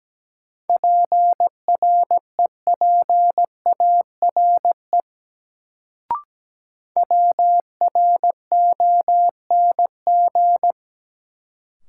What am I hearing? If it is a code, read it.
Morse code, heard as a keyed tone, about 17 wpm, 700 Hz. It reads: PREPARE WRONG
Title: Sentences from Top 900 Words - New Word Drill - 17wpm